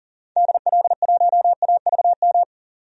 Morse code, heard as a keyed tone, about 40 wpm, 700 Hz. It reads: DL1AVM
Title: DXCC-100 Call Signs - 6 Characters - 40wpm